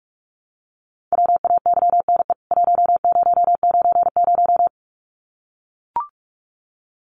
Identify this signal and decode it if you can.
Morse code, heard as a keyed tone, about 44 wpm, 700 Hz. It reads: WAYNE 1090
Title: CWT Exchange 45wpm